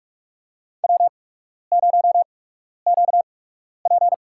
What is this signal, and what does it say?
Morse code, heard as a keyed tone, about 43 wpm, spaced as for 12 wpm, 700 Hz. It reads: W0QP
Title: USA Call Signs - 4 Characters - 8x Character Spacing - 45wpm